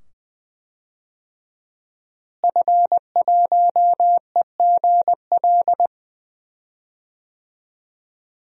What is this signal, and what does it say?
Morse code, heard as a keyed tone, about 20 wpm, 700 Hz. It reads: F1EGL
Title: DXCC-100 Call Signs - 5 Characters - Repeated 2x - 20wpm